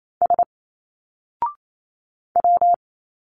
Morse code, heard as a keyed tone, about 27 wpm, 700 Hz. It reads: S W
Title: Single Letters 28wpm